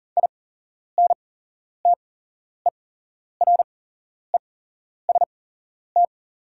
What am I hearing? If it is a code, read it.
Morse code, heard as a keyed tone, about 39 wpm, spaced as for 11 wpm, 690 Hz. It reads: INTEREST
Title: 8x Character Spacing - Top 500 Words - 40wpm